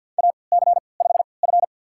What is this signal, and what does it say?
Morse code, heard as a keyed tone, about 52 wpm, spaced as for 28 wpm, 700 Hz. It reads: AC5F
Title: USA Call Signs - 4 Characters - 3x Character Spacing - 50wpm